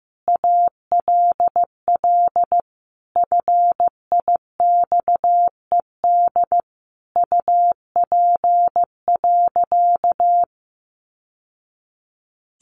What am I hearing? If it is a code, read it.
Morse code, heard as a keyed tone, about 14 wpm, 700 Hz. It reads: ALL FIXED UP.
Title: Sentences from Top 1100 Words 15wpm